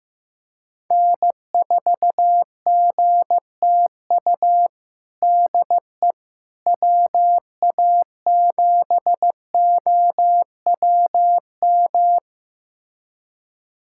Morse code, heard as a keyed tone, about 14 wpm, 700 Hz. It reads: N4GTU DE WA7OWM
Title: USA Callsign DE Callsign 15wpm